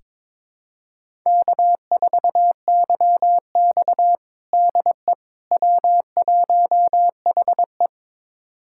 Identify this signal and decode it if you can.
Morse code, heard as a keyed tone, about 22 wpm, 700 Hz. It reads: K4YX DE W1HE
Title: USA Callsign DE Callsign - 4 Characters - 22wpm